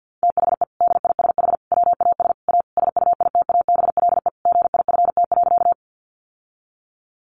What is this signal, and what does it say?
Morse code, heard as a keyed tone, about 50 wpm, 700 Hz. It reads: THE DISH WAS A SUITABLE GIFT.
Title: Sentences from Top 1400 Words 50wpm